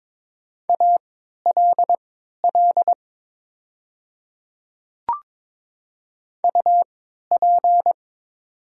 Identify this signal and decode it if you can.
Morse code, heard as a keyed tone, about 22 wpm, spaced as for 12 wpm, 700 Hz. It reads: ALL UP
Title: 3x Character Spacing - Top 100 Words - 22wpm